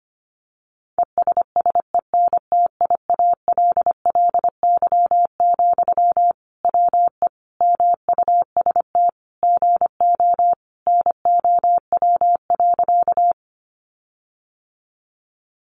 Morse code, heard as a keyed tone, about 25 wpm, 700 Hz. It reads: ESSENTIALLY, WE MUST GO NOW.
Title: Sentences from Top 1300 Words 25wpm